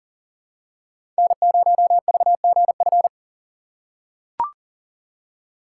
Morse code, heard as a keyed tone, about 40 wpm, 700 Hz. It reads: N0VGF
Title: USA Call Signs - 4x Character Spacing - 40wpm